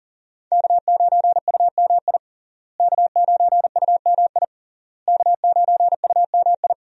Morse code, heard as a keyed tone, about 40 wpm, 700 Hz. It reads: K9UMI K9UMI K9UMI
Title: USA Call Signs - 5 Characters - Repeated 3x - 40wpm